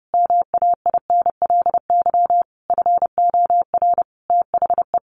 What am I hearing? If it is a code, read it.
Morse code, heard as a keyed tone, about 30 wpm, 700 Hz. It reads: MAINLY FOR THE
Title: Sentences from Top 1400 Words 30wpm